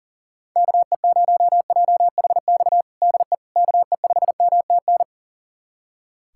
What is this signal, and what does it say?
Morse code, heard as a keyed tone, about 37 wpm, 700 Hz. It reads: KE0JHX DE KE5MTN
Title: USA Callsign DE Callsign - 6 Characters - 40wpm